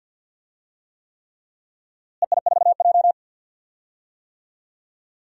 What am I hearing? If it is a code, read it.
Morse code, heard as a keyed tone, about 46 wpm, 700 Hz. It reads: EI4J
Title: DXCC-100 Call Signs - 4 Characters - Repeated 3x - 50wpm